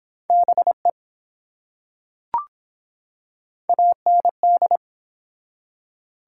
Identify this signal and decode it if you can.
Morse code, heard as a keyed tone, about 23 wpm, 700 Hz. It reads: BE AND